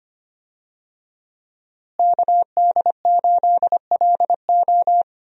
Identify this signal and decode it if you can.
Morse code, heard as a keyed tone, about 25 wpm, 700 Hz. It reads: KD8LO
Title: USA Call Signs - 5 Characters - Repeated 3x - 25wpm